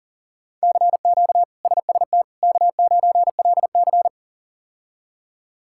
Morse code, heard as a keyed tone, about 40 wpm, 700 Hz. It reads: CQ SST K9LC
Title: K1USN SST QSO Practice - 40wpm